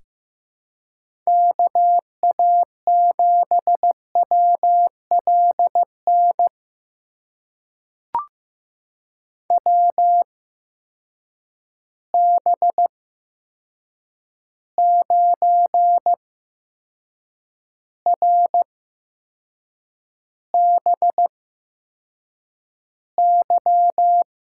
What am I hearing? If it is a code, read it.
Morse code, heard as a keyed tone, about 15 wpm, spaced as for 6 wpm, 700 Hz. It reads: KA7WLN WB9RBY
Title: USA Call Signs - 6 Characters - 8x Character Spacing - 15wpm